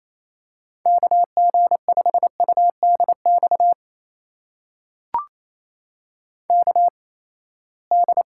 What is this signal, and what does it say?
Morse code, heard as a keyed tone, about 28 wpm, spaced as for 19 wpm, 700 Hz. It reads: KG5UDX KD
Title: USA Call Signs - 6 Characters - 8x Character Spacing - 28wpm